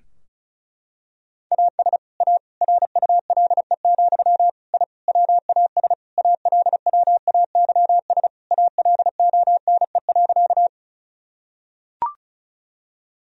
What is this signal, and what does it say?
Morse code, heard as a keyed tone, about 36 wpm, 700 Hz. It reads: AS A RULE, I WAS ALWAYS ALONE.
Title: Sentences from Top 1000 Words - Review Part 2 - 35wpm